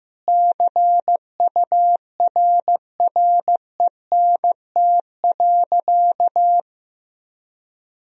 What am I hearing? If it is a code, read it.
Morse code, heard as a keyed tone, about 16 wpm, 700 Hz. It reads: CURRENT.
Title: Sentences from Top 1000 Words - Review Part 4 - 15wpm